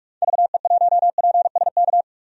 Morse code, heard as a keyed tone, about 46 wpm, 700 Hz. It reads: VE1PSK
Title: DXCC-100 Call Signs - 6 Characters - 45wpm